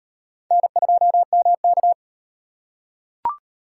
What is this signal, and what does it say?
Morse code, heard as a keyed tone, about 36 wpm, 700 Hz. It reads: N2MK